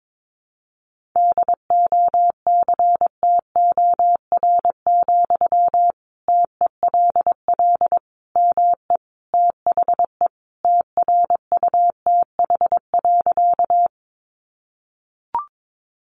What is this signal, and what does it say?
Morse code, heard as a keyed tone, about 22 wpm, 700 Hz. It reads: DOCTOR, TELL ME THE TRUTH.
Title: Sentences from Top 1500 Words 22wpm